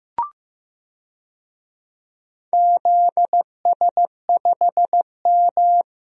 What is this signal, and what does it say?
Morse code, heard as a keyed tone, about 15 wpm, 700 Hz. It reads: ZS5M
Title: DXCC-100 Call Signs - 4 Characters - Repeated 2x - 15wpm